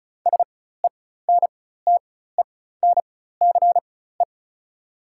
Morse code, heard as a keyed tone, about 34 wpm, spaced as for 16 wpm, 700 Hz. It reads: SENTENCE
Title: 4x Character Spacing - Top 300 Words - 35wpm